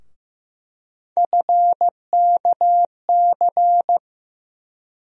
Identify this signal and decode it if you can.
Morse code, heard as a keyed tone, about 15 wpm, 700 Hz. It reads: FKC
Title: USA Call Sign Postfixes - 3 Letters 15wpm